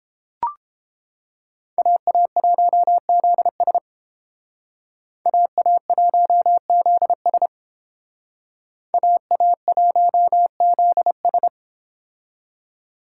Morse code, heard as a keyed tone, about 30 wpm, 700 Hz. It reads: AA1ZS AA1ZS AA1ZS